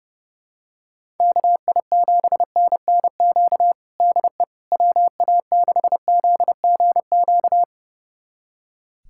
Morse code, heard as a keyed tone, about 30 wpm, 700 Hz. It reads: KI7NNQ DE WA6ZGQ
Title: USA Callsign DE Callsign - 6 Characters - 30wpm